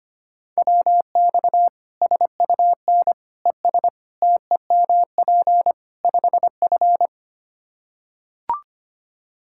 Morse code, heard as a keyed tone, about 25 wpm, 700 Hz. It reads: WX SUN ES TEMP 5F
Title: Practice QSO - Exchange 2 - Info - 25wpm